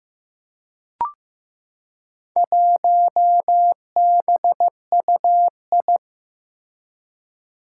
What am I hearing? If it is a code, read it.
Morse code, heard as a keyed tone, about 15 wpm, 700 Hz. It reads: 1BUI
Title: Four Challenging Letter-Number Combinations 15wpm